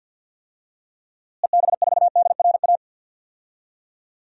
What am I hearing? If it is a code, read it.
Morse code, heard as a keyed tone, about 50 wpm, 700 Hz. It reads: EB4DRA
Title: DXCC-100 Call Signs - 6 Characters - Repeated 3x - 50wpm